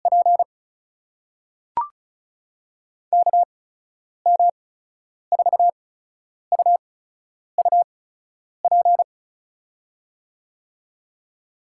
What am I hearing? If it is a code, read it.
Morse code, heard as a keyed tone, about 36 wpm, spaced as for 10 wpm, 700 Hz. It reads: P KM4UUP
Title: USA Call Signs - 6 Characters - 8x Character Spacing - 35wpm